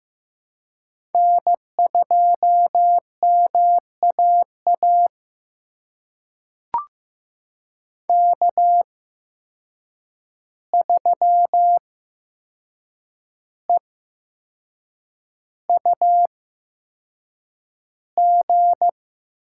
Morse code, heard as a keyed tone, about 15 wpm, spaced as for 6 wpm, 700 Hz. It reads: N2MAA K3EUG
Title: USA Call Signs - 5 Characters - 8x Character Spacing - 15wpm